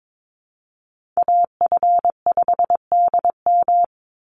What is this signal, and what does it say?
Morse code, heard as a keyed tone, about 22 wpm, 700 Hz. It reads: AF5DM
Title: Call Signs - 22wpm